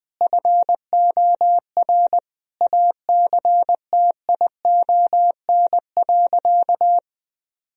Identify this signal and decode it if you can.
Morse code, heard as a keyed tone, about 21 wpm, 700 Hz. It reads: FOR ACTION.
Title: Sentences from Top 1000 Words - Review Part 4 - 20wpm